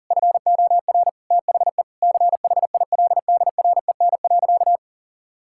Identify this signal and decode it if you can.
Morse code, heard as a keyed tone, about 40 wpm, 690 Hz. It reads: FOR THE CHILDREN.
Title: Sentences from Top 1000 Words - Review Part 6 - 40wpm